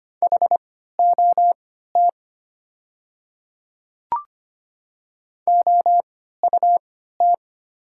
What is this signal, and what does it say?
Morse code, heard as a keyed tone, about 25 wpm, spaced as for 14 wpm, 700 Hz. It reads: HOT OUT
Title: 3x Character Spacing - Top 100 Words - 25wpm